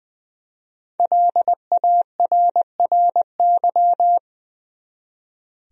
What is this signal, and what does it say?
Morse code, heard as a keyed tone, about 20 wpm, 700 Hz. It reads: LARRY
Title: USA Common Names - 20wpm